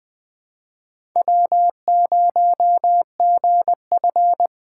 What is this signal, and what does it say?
Morse code, heard as a keyed tone, about 20 wpm, 700 Hz. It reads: W0GF
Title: USA Call Signs - 4 Characters - 6x Character Spacing - 20wpm